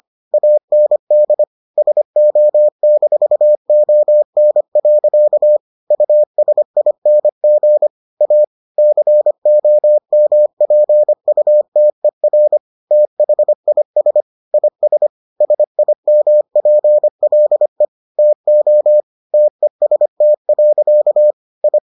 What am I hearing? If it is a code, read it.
Morse code, heard as a keyed tone, about 25 wpm, 600 Hz. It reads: AND SO-ON. USING A COMPUTER THIS IS SIMPLE TO TEST. I